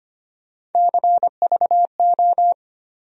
Morse code, heard as a keyed tone, about 25 wpm, 700 Hz. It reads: CVO
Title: USA Call Sign Postfixes - 3 Letters 25wpm